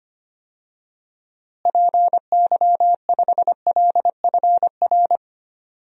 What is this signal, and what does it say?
Morse code, heard as a keyed tone, about 25 wpm, 700 Hz. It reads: PY5LFR
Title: DXCC-100 Call Signs - 6 Characters - Repeated 2x - 25wpm